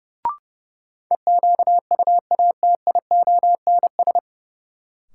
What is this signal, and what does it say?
Morse code, heard as a keyed tone, about 30 wpm, 700 Hz. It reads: EQUATIONS
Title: Sets of 5 Words - New Word Drill - 30wpm